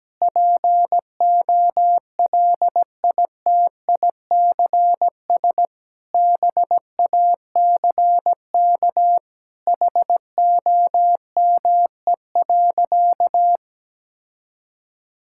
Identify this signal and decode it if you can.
Morse code, heard as a keyed tone, about 17 wpm, 700 Hz. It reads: POLITICS BACK HOME.
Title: Sentences from Top 1000 Words - Review Part 6 - 17wpm